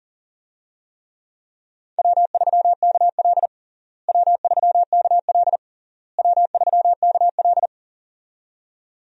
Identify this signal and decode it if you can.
Morse code, heard as a keyed tone, about 40 wpm, 700 Hz. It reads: W3KL W3KL W3KL
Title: USA Call Signs - 4 Characters - Repeated 3x - 40wpm